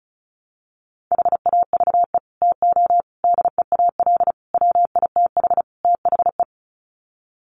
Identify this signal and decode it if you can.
Morse code, heard as a keyed tone, about 34 wpm, 700 Hz. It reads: HAVE TO DEAL WITH THE
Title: Sets of 5 Words Encore2 35wpm